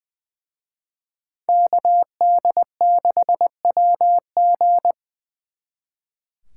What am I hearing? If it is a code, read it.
Morse code, heard as a keyed tone, about 24 wpm, 700 Hz. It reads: KD6WG
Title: Speed Racing - USA 2x2 Call Signs 20wpm